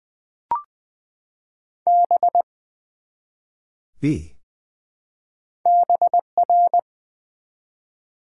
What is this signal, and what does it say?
Morse code, heard as a keyed tone, about 20 wpm, 700 Hz. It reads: B BR
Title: Build A Word - 1 Letter At A Time - 5-Letter Words - 20wpm